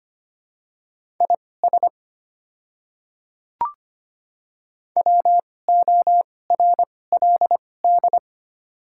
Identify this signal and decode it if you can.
Morse code, heard as a keyed tone, about 25 wpm, spaced as for 18 wpm, 700 Hz. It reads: IS WORLD
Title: 2x Character Spacing - Top 300 Words - 25wpm